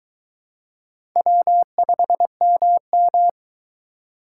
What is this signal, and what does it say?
Morse code, heard as a keyed tone, about 22 wpm, 700 Hz. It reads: W5MM